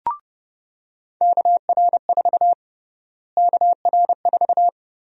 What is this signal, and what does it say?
Morse code, heard as a keyed tone, about 30 wpm, 700 Hz. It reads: KR4 KR4
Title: USA Call Sign Prefixes - Repeated 2x - 30wpm